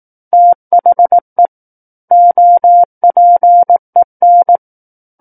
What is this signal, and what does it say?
Morse code, heard as a keyed tone, about 18 wpm, 700 Hz. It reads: THE OPEN
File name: 18 WPM